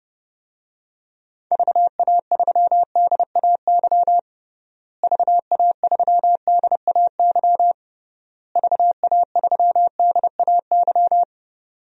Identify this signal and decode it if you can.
Morse code, heard as a keyed tone, about 30 wpm, 700 Hz. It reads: VA3DAY VA3DAY VA3DAY
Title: DXCC-100 Call Signs - 6 Characters - Repeated 3x - 30wpm